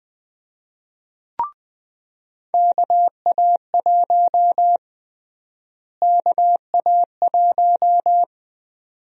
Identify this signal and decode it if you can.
Morse code, heard as a keyed tone, about 20 wpm, 700 Hz. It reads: KA1 KA1
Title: USA Call Sign Prefixes - Repeated 2x - 20wpm